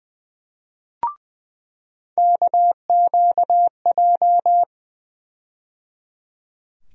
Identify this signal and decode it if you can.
Morse code, heard as a keyed tone, about 20 wpm, 700 Hz. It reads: KQJ
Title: USA Call Sign Postfixes - 3 Letters 20wpm